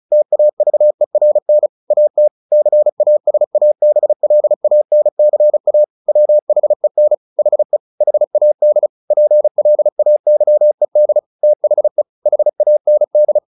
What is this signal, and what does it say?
Morse code, heard as a keyed tone, about 34 wpm, 600 Hz. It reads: TAVERN AT CASABLANCA WHEN HE HAD PLAYED THE HAND
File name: the old man and the sea / the old man and the sea 1 35wpm